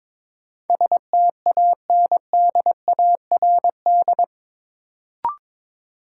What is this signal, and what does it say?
Morse code, heard as a keyed tone, about 22 wpm, 700 Hz. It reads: STANDARD
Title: Sentences from Top 1000 Words - Review Part 6 - New Word Drill - 22wpm